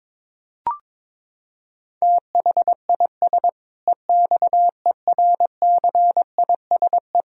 Morse code, heard as a keyed tone, about 22 wpm, 700 Hz. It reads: THIS EXERCISE
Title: Sentences from Top 1000 Words - Review Part 4 - 22wpm